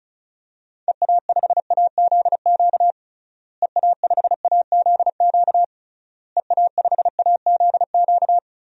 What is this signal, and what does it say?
Morse code, heard as a keyed tone, about 36 wpm, 700 Hz. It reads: EA5AZQ EA5AZQ EA5AZQ
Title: DXCC-100 Call Signs - 6 Characters - Repeated 3x - 35wpm